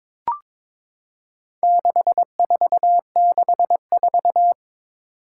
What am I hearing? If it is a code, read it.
Morse code, heard as a keyed tone, about 22 wpm, 700 Hz. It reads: 6464
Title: Four Challenging Letter-Number Combinations 22wpm